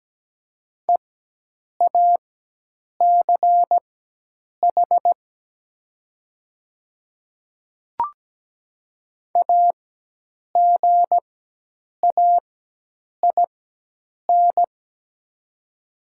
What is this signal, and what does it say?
Morse code, heard as a keyed tone, about 17 wpm, spaced as for 8 wpm, 700 Hz. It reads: EACH AGAIN
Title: 4x Character Spacing - Top 200 Words - 17wpm